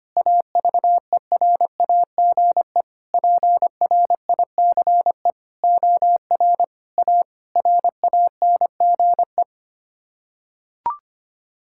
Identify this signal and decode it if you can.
Morse code, heard as a keyed tone, about 25 wpm, 700 Hz. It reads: AVERAGE PRICE OR A RANGE
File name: Sets of 9 Words - 25wpm